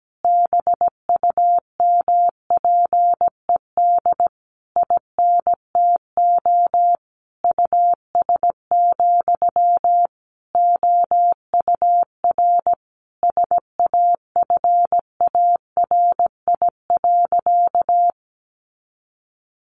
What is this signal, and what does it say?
Morse code, heard as a keyed tone, about 17 wpm, 700 Hz. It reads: BUMPED INTO US, OUR SAFARI.
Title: A Sound Of Thunder 17wpm